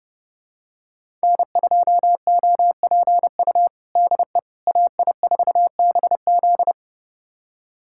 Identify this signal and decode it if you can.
Morse code, heard as a keyed tone, about 30 wpm, 700 Hz. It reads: N2OPU DE AI4BZ
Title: USA Callsign DE Callsign 30wpm